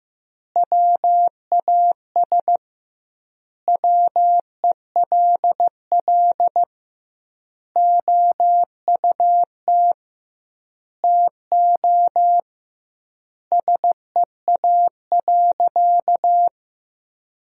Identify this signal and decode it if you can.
Morse code, heard as a keyed tone, about 15 wpm, 700 Hz. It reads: WAS WELL OUT TO SEA.
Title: 2x Word-Spacing - Sentences from Top 400 Words - 15wpm